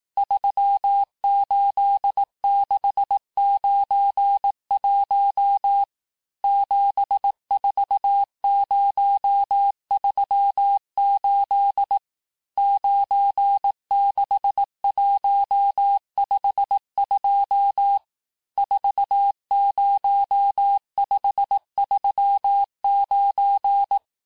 Morse code, heard as a keyed tone, about 18 wpm, 800 Hz. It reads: 38691 74038 96152 40539
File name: cours-du-lundi files / cours-1588-texte-chiffre-vitesse-17MPM